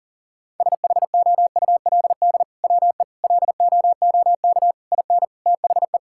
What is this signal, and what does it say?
Morse code, heard as a keyed tone, about 37 wpm, 700 Hz. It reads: SHOULD WE LOOK IN THE
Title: Sentences from Top 1600 Words 40wpm